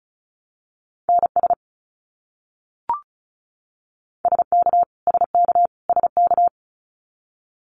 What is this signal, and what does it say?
Morse code, heard as a keyed tone, about 34 wpm, 700 Hz. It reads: NS SK SK SK
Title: Canadian Provinces and Territories - Repeated 3x - 35wpm